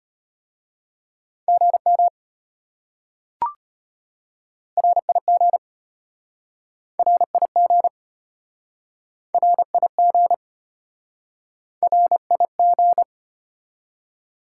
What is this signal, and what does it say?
Morse code, heard as a keyed tone, about 33 wpm, 700 Hz. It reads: GM RIG RIG RIG RIG